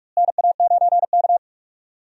Morse code, heard as a keyed tone, about 42 wpm, 700 Hz. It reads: NA9K
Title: USA Call Signs - 4 Characters - Repeated 2x - 45wpm